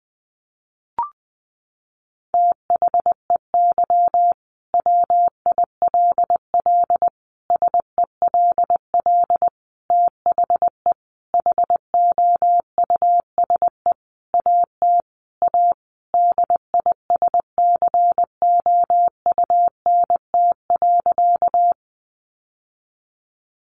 Morse code, heard as a keyed tone, about 20 wpm, 700 Hz. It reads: THEY WILL SELL THE HOUSE AT A DISCOUNT.
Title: Sentences from Top 1200 Words 20wpm